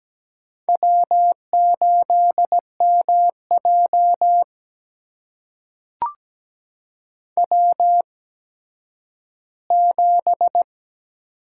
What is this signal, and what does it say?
Morse code, heard as a keyed tone, about 17 wpm, spaced as for 10 wpm, 700 Hz. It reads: W8MJ W7
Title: USA Call Signs - 4 Characters - 8x Character Spacing - 17wpm